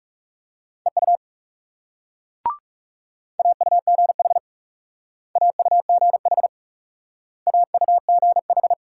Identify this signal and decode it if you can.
Morse code, heard as a keyed tone, about 38 wpm, 700 Hz. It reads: EU AUGH AUGH AUGH